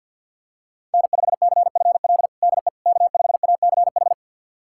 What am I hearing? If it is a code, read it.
Morse code, heard as a keyed tone, about 51 wpm, 700 Hz. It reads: N5CFL DE K5ACH
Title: USA Callsign DE Callsign - 5 Characters - 50wpm